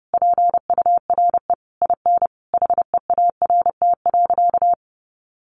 Morse code, heard as a keyed tone, about 30 wpm, 700 Hz. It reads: PURE IN HEART.
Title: Sentences from Top 1500 Words 30wpm